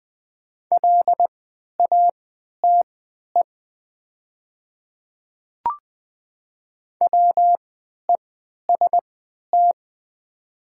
Morse code, heard as a keyed tone, about 20 wpm, spaced as for 11 wpm, 700 Hz. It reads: LATE WEST
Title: 3x Character Spacing - Top 500 Words - 20wpm